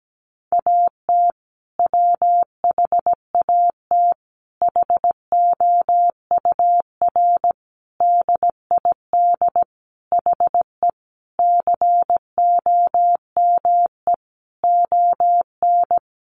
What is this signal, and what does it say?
Morse code, heard as a keyed tone, about 17 wpm, 700 Hz. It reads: AT WHAT HOUR DID HE COME ON
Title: Sentences from Top 700 Words 17wpm